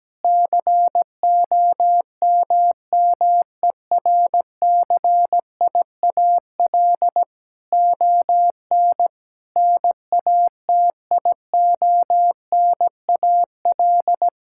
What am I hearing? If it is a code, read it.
Morse code, heard as a keyed tone, about 17 wpm, 700 Hz. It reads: COMMERCIAL ON NATIONAL
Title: Sentences from Top 1000 Words - Review Part 5 - 17wpm